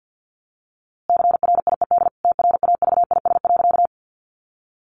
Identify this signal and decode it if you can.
Morse code, heard as a keyed tone, about 50 wpm, 700 Hz. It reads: CRIED TRAVIS.
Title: A Sound Of Thunder 50wpm